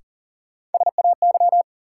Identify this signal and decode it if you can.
Morse code, heard as a keyed tone, about 40 wpm, 700 Hz. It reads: SAY
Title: Sets of 2 Words - New Word Drill - 40wpm